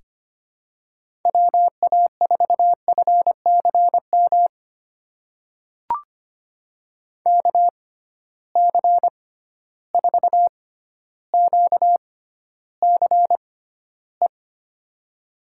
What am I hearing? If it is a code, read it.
Morse code, heard as a keyed tone, about 25 wpm, spaced as for 13 wpm, 700 Hz. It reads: WA4FCM KC4QCE
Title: USA Call Signs - 6 Characters - 6x Character Spacing - 25wpm